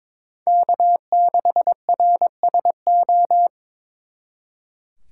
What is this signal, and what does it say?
Morse code, heard as a keyed tone, about 26 wpm, 700 Hz. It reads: K6RSO
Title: Speed Racing - USA 1x3 Call Signs 22wpm